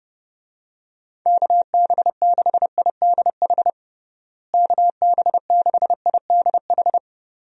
Morse code, heard as a keyed tone, about 30 wpm, 700 Hz. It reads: KB6IDH KB6IDH
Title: USA Call Signs - 6 Characters - Repeated 2x - 30wpm